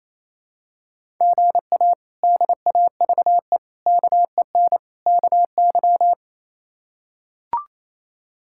Morse code, heard as a keyed tone, about 28 wpm, 700 Hz. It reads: GA DAVE KEN KY
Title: K1USN SST QSO Practice - 28wpm